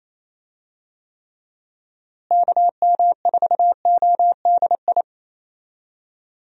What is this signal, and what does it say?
Morse code, heard as a keyed tone, about 28 wpm, 700 Hz. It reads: KM4ODI
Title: USA Call Signs - 6 Characters - Repeated 2x - 28wpm